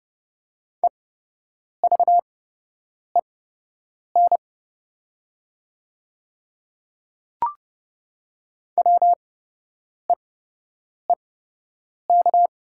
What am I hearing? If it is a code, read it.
Morse code, heard as a keyed tone, about 30 wpm, spaced as for 8 wpm, 700 Hz. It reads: EVEN WEEK
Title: 8x Character Spacing - Top 500 Words - 30wpm